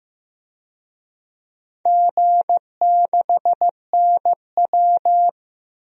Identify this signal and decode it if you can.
Morse code, heard as a keyed tone, about 15 wpm, 700 Hz. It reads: G6NW
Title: DXCC-100 Call Signs - 4 Characters - Repeated 2x - 15wpm